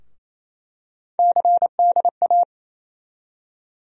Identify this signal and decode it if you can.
Morse code, heard as a keyed tone, about 27 wpm, 700 Hz. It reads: CDA
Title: USA Call Sign Postfixes - 3 Letters 28wpm